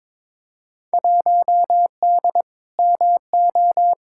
Morse code, heard as a keyed tone, about 22 wpm, 700 Hz. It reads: 1D MO
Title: Field Day Exchanges - 22wpm